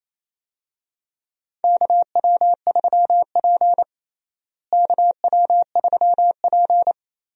Morse code, heard as a keyed tone, about 27 wpm, 700 Hz. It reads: KW3P KW3P
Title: USA Call Signs - 4 Characters - Repeated 3x - 28wpm